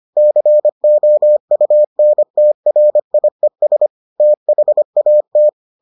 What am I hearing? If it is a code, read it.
Morse code, heard as a keyed tone, about 25 wpm, 600 Hz. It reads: COUNTRIES THAT